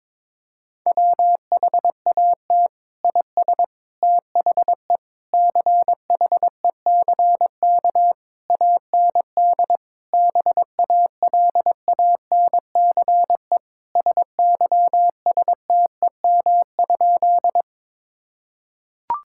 Morse code, heard as a keyed tone, about 22 wpm, 700 Hz. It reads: WHAT IS THE CHECK AND BALANCE SYSTEM?
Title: Sentences from Top 1000 Words - Review Part 6 - 22wpm